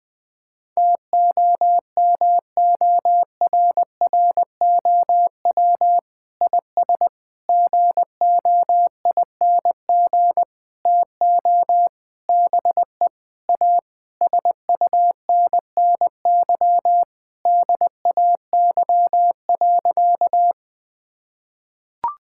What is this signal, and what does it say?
Morse code, heard as a keyed tone, about 21 wpm, 700 Hz. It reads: TOMORROW IS GOING TO BE A SUNNY DAY.
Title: Sentences with 5-Word Phrases - 20wpm